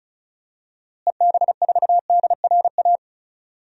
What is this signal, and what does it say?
Morse code, heard as a keyed tone, about 34 wpm, 700 Hz. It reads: EB4DRA
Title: DXCC-100 Call Signs - 6 Characters - Repeated 3x - 35wpm